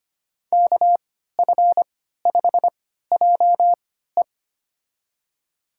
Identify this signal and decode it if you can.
Morse code, heard as a keyed tone, about 25 wpm, spaced as for 14 wpm, 700 Hz. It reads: KF5JE
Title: USA Call Signs - 5 Characters - 3x Character Spacing - 25wpm